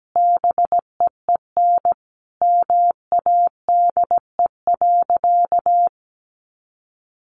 Morse code, heard as a keyed tone, about 17 wpm, 700 Hz. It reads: BEEN MADE.
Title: Sentences from Top 1400 Words 17wpm